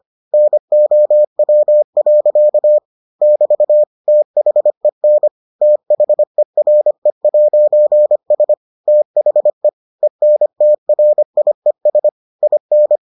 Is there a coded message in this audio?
Yes: Morse code from 0.3 to 13.0 s, heard as a keyed tone, about 25 wpm, 600 Hz. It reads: NOW. <BT> THEN THERE'S THE ENTRIES IN